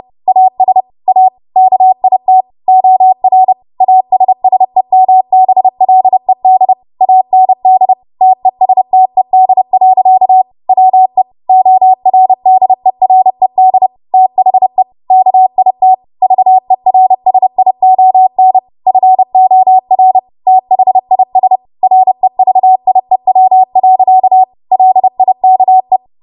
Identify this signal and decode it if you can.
Morse code, heard as a keyed tone, about 30 wpm, 750 Hz. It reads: AS A KIT OR ASSEMBLED AND TESTED. WE ORDERED THE KIT VERSION FOR THIS REVIEW. LIKE